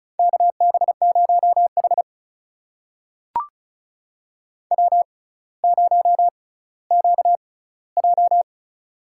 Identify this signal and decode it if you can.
Morse code, heard as a keyed tone, about 36 wpm, spaced as for 18 wpm, 700 Hz. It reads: KB0H W0QJ
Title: USA Call Signs - 4 Characters - 6x Character Spacing - 35wpm